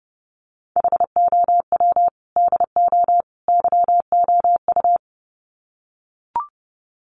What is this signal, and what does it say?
Morse code, heard as a keyed tone, about 30 wpm, 700 Hz. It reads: HOW DO YOU
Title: Sets of 3 Words 30wpm